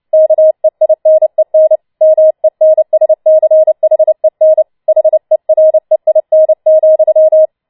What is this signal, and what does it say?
Morse code, heard as a keyed tone, about 29 wpm, 630 Hz. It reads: KEINEN MENSCHEN HEREIN,
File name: Schneewittchen Teil3 30wpm